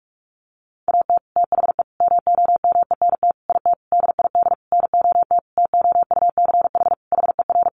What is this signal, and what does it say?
Morse code, heard as a keyed tone, about 44 wpm, 700 Hz. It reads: AT THE MOMENT IT DID NOT TOUCH HER
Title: Sentences from Top 1200 Words 45wpm